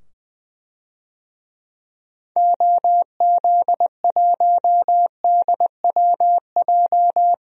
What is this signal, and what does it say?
Morse code, heard as a keyed tone, about 20 wpm, 700 Hz. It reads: OZ1DWJ
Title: DXCC-100 Call Signs - 6 Characters - 20wpm